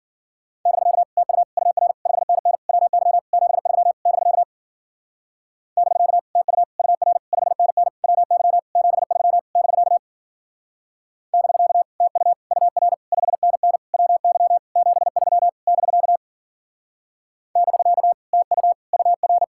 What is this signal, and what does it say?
Morse code, heard as a keyed tone, about 48 wpm, 700 Hz. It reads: <BK> TU UR 5NN WY 73 <BK> <BK> TU UR 5NN WY 73 <BK> <BK> TU UR 5NN WY 73 <BK> <BK> TU UR
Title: Speed-Racing Common QSO Phrases Encore - 40wpm